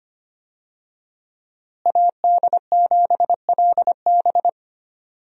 Morse code, heard as a keyed tone, about 25 wpm, 700 Hz. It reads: AD7LB
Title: USA Call Signs - 5 Characters - Repeated 3x - 25wpm